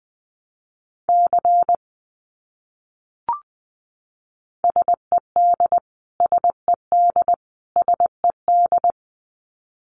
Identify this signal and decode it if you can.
Morse code, heard as a keyed tone, about 20 wpm, 700 Hz. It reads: C SED SED SED
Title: Abbreviations Repeated 3x - 20wpm